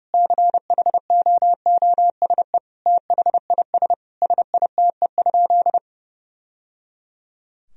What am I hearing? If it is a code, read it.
Morse code, heard as a keyed tone, about 30 wpm, 700 Hz. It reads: CHOOSE THIS SITE?
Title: Sentences from Top 1000 Words - Review Part 4 - 30wpm